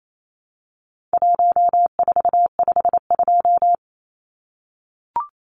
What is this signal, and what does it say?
Morse code, heard as a keyed tone, about 28 wpm, 700 Hz. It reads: 1452
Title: Numbers - Contest Style - 4-Digit 28wpm